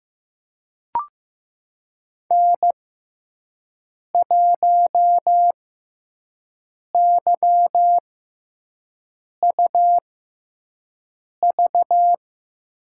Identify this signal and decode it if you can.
Morse code, heard as a keyed tone, about 15 wpm, spaced as for 5 wpm, 700 Hz. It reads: N1YUV
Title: USA Call Signs - 5 Characters - 6x Character Spacing - 15wpm